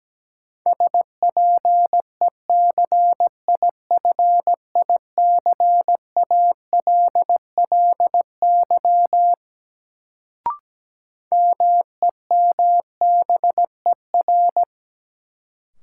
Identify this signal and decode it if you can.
Morse code, heard as a keyed tone, about 17 wpm, 700 Hz. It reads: SPECIFICALLY MEMBER
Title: Sentences from Top 1100 Words - New Word Drill Alt - 17wpm